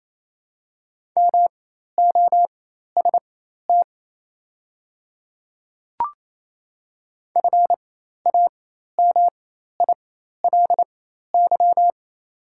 Morse code, heard as a keyed tone, about 28 wpm, spaced as for 13 wpm, 700 Hz. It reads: MOST FAMILY
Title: 4x Character Spacing - Top 500 Words - 28wpm